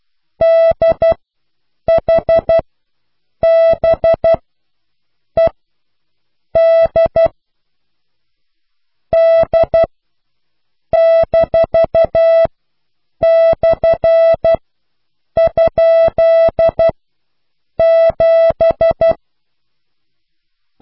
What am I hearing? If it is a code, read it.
Morse code, heard as a keyed tone, about 12 wpm, spaced as for 7 wpm, 660 Hz. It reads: DHBED D-/?7